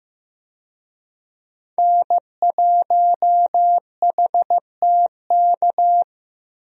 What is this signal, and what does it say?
Morse code, heard as a keyed tone, about 15 wpm, 700 Hz. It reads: N1HTK
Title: USA Call Signs - 5 Characters - Repeated 3x - 15wpm